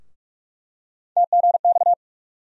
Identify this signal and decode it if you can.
Morse code, heard as a keyed tone, about 42 wpm, 700 Hz. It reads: TGX
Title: USA Call Sign Postfixes - 3 Letters 45wpm